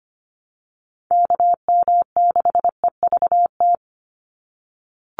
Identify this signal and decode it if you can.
Morse code, heard as a keyed tone, about 25 wpm, 700 Hz. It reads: KM6EVT
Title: USA Call Signs 25wpm